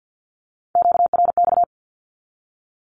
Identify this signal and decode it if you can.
Morse code, heard as a keyed tone, about 50 wpm, 700 Hz. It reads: QRX
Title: Q-Codes with Spoken Meaning - 50wpm